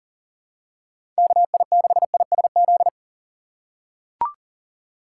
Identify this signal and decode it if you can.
Morse code, heard as a keyed tone, about 39 wpm, 700 Hz. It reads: KI6ISZ
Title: USA Call Signs - 6 Characters - 2x Character Spacing - 40wpm